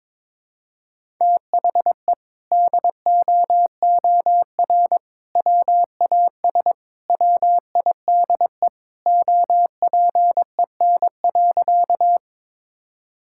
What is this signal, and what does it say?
Morse code, heard as a keyed tone, about 21 wpm, 700 Hz. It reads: THE DOOR WAS WIDE OPEN.
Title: Sentences from Top 1000 Words - Review Part 6 - 22wpm